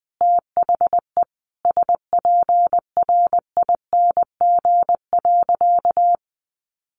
Morse code, heard as a keyed tone, about 20 wpm, 700 Hz. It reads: THE SPRING.
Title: Sentences from Top 1200 Words 20wpm